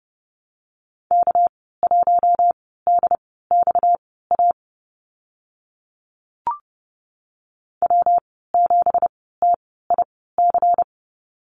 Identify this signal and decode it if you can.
Morse code, heard as a keyed tone, about 30 wpm, spaced as for 17 wpm, 700 Hz. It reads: K1DXA W7TIC
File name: Lesson 37g - Letter x - Call Signs 30wpm